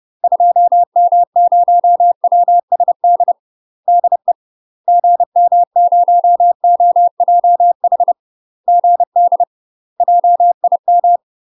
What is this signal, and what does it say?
Morse code, heard as a keyed tone, about 30 wpm, 700 Hz. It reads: 2M0WSD DE GM0OJH GD JIM
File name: UK MASTER QSOs TRAINING / 30WPM / UK MASTER QSOs 0048